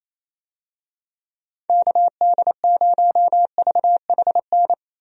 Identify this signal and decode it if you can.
Morse code, heard as a keyed tone, about 27 wpm, 700 Hz. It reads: KD0VHN
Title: USA Call Signs - 6 Characters - Repeated 2x - 28wpm